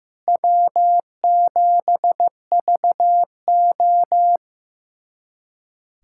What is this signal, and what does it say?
Morse code, heard as a keyed tone, about 15 wpm, 700 Hz. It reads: W7VO
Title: USA Callsign DE Callsign - 4 Characters - 15wpm